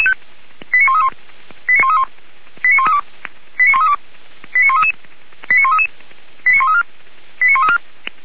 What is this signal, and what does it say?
Five-tone selective call (ZVEI), heard as quick runs of five tones, tones of about 70 ms. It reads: 89131 89132 89132 89133 89133 89134 89134